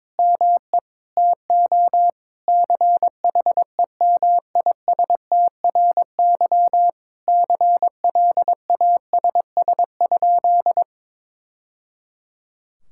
Morse code, heard as a keyed tone, about 22 wpm, 700 Hz. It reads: ME TO CHEMISTRY CLASS?
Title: Sentences from Top 1000 Words - Review Part 6 - 22wpm